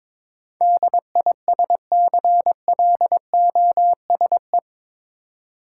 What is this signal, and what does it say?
Morse code, heard as a keyed tone, about 22 wpm, 700 Hz. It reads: DISCLOSE
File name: practice / 22wpm / Eight Letter Words - 22wpm